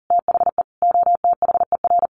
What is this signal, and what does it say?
Morse code, heard as a keyed tone, about 40 wpm, 700 Hz. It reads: THE OTHER
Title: Sentences from Top 1300 Words 40wpm